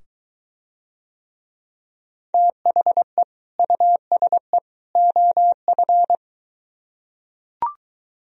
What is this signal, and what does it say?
Morse code, heard as a keyed tone, about 23 wpm, 700 Hz. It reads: THE USE OF